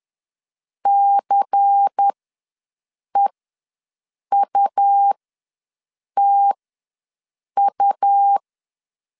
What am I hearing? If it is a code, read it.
Morse code, heard as a keyed tone, about 10 wpm, spaced as for 6 wpm, 800 Hz. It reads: CEUTU
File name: LEZIONI / 5 LBVC / 5 LBVC ws 10wpm 3733